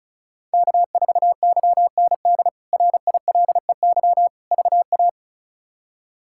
Morse code, heard as a keyed tone, about 36 wpm, 700 Hz. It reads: K4YND RILEY VA
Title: K1USN SST QSO Practice - 35wpm